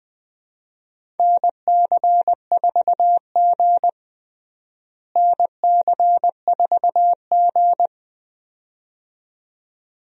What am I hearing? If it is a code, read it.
Morse code, heard as a keyed tone, about 20 wpm, 700 Hz. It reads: NC4G NC4G
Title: USA Call Signs - 4 Characters - Repeated 2x - 20wpm